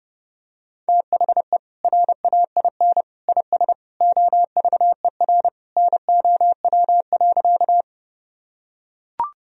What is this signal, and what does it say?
Morse code, heard as a keyed tone, about 32 wpm, 700 Hz. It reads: THE RAIN IS OVER NOW.
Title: Sentences from Top 1000 Words - Review Part 1 - 30wpm